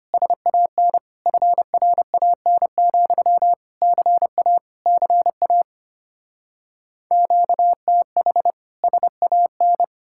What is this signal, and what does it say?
Morse code, heard as a keyed tone, about 33 wpm, 700 Hz. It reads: SAN FRAN, CA CA QTH SAN